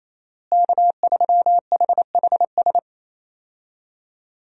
Morse code, heard as a keyed tone, about 28 wpm, 700 Hz. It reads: K3HHS
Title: USA Call Signs - 5 Characters - Repeated 3x - 28wpm